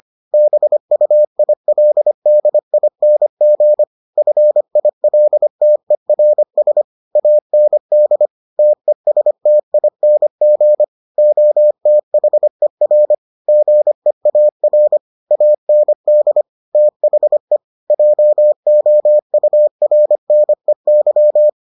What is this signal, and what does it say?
Morse code, heard as a keyed tone, about 25 wpm, 600 Hz. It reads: BUILDING FILTERS AND TESTING OTHER GEAR AND THE JOURNEY